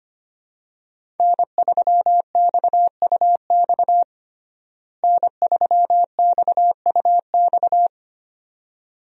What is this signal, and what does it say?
Morse code, heard as a keyed tone, about 25 wpm, 700 Hz. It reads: N3XUX N3XUX
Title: USA Call Signs - 5 Characters - Repeated 2x - 25wpm